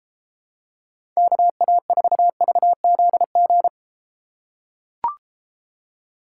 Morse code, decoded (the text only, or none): KA4VZG